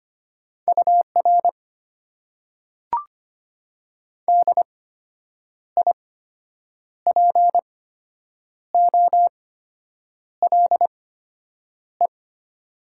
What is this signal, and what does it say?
Morse code, heard as a keyed tone, about 25 wpm, spaced as for 8 wpm, 700 Hz. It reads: UR DIPOLE
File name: Top 90 QSO Elements - 8x Character Spacing - 25wpm